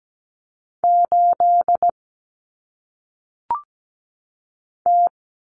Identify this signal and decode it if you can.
Morse code, heard as a keyed tone, about 17 wpm, 700 Hz. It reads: 8 T
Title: Common Cut Numbers - 1-Digit 17wpm